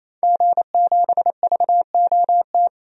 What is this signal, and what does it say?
Morse code, heard as a keyed tone, about 27 wpm, 700 Hz. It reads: G7VOT
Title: DXCC-100 Call Signs - 5 Characters - 28wpm